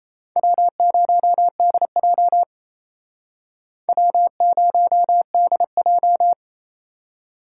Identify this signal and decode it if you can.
Morse code, heard as a keyed tone, about 35 wpm, 700 Hz. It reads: W0DJ W0DJ